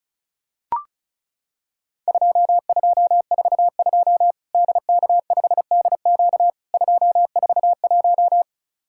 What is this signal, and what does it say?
Morse code, heard as a keyed tone, about 36 wpm, 700 Hz. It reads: 2242 DK5DQ 241
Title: Worked All Europe DX Contest - QTC Exchange - 35wpm